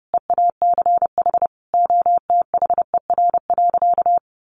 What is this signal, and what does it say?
Morse code, heard as a keyed tone, about 30 wpm, 700 Hz. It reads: EACH OTHER.
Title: Easy Phrases 30wpm